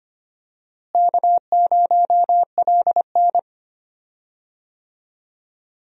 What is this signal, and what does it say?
Morse code, heard as a keyed tone, about 25 wpm, 700 Hz. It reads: K0LN
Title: USA Call Signs - 4 Characters - Repeated 3x - 25wpm